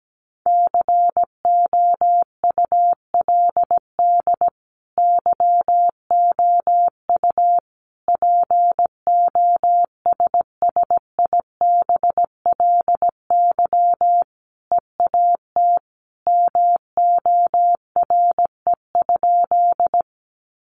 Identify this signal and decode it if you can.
Morse code, heard as a keyed tone, about 17 wpm, 700 Hz. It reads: COULD YOU POSSIBLY EAT MORE?
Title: Sentences from Top 1300 Words 17wpm